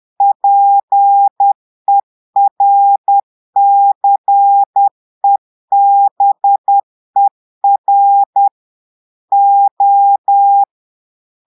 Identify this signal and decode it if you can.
Morse code, heard as a keyed tone, about 10 wpm, 800 Hz. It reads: PERCEBER O
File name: portugais texte en clair 2 vitesse 10 mots minutes